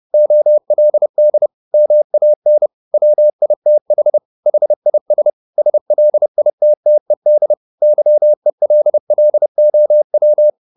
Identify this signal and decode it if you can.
Morse code, heard as a keyed tone, about 30 wpm, 600 Hz. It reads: OLD MAN WITH HIS SLITTED YELLOW